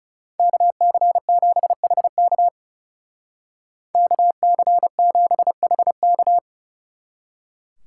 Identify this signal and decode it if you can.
Morse code, heard as a keyed tone, about 35 wpm, 700 Hz. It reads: KC7HK KC7HK
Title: Speed Racing - USA 2x2 Call Signs 30wpm